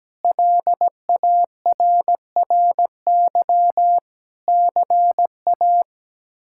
Morse code, heard as a keyed tone, about 17 wpm, 700 Hz. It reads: LARRY CA
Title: K1USN SST QSO Practice - 17wpm